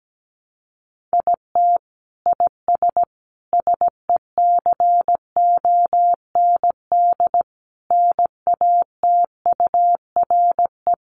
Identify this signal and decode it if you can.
Morse code, heard as a keyed tone, about 17 wpm, 700 Hz. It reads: IT IS SECOND NATURE
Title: Sentences from Top 600 Words 17wpm